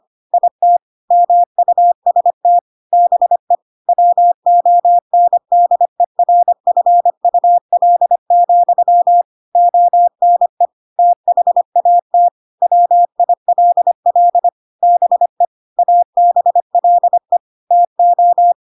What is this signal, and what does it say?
Morse code, heard as a keyed tone, about 25 wpm, 700 Hz. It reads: IT MUST BE WONDERFUL, ONE THAT WILL BE ABLE TO
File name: Pinocchio / Pinocchio ch2 25 25 ik1hge